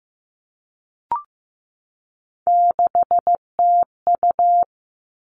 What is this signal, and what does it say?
Morse code, heard as a keyed tone, about 15 wpm, 700 Hz. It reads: 6TU
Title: All Cut Numbers - 3-Digit 15wpm